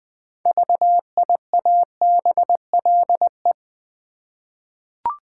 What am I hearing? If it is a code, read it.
Morse code, heard as a keyed tone, about 20 wpm, 700 Hz. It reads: VIABLE
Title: 6 Letter Words - First 3 Letters Given - 20wpm